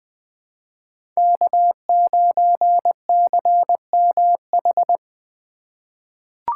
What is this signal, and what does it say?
Morse code, heard as a keyed tone, about 21 wpm, 700 Hz. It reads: K9CMH
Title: USA Call Signs - 5 Characters - 6x Character Spacing - 20wpm